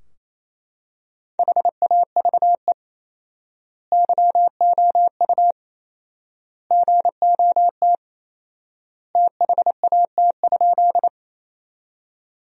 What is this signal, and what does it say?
Morse code, heard as a keyed tone, about 28 wpm, 700 Hz. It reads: HAVE YOU GOT THAT?
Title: 4x Word-Spacing - Sentences from Top 300 Words - 28wpm